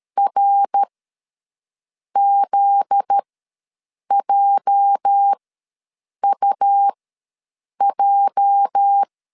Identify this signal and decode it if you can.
Morse code, heard as a keyed tone, about 13 wpm, spaced as for 6 wpm, 800 Hz. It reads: RZJUJ
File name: LEZIONI / 8 YJ45 / 8 YJ ws 12wpm 3342